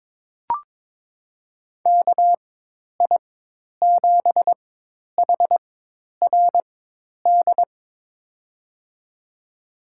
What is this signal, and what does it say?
Morse code, heard as a keyed tone, about 22 wpm, spaced as for 10 wpm, 700 Hz. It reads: KI7HRD
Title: USA Call Signs - 6 Characters - 4x Character Spacing - 22wpm